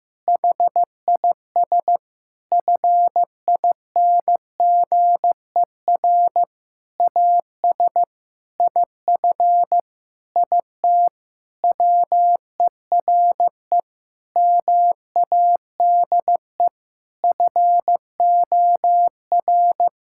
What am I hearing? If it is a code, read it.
Morse code, heard as a keyed tone, about 16 wpm, 700 Hz. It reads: HIS FINGER AS IF IT WERE MADE FOR
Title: Sentences from Top 1600 Words 15wpm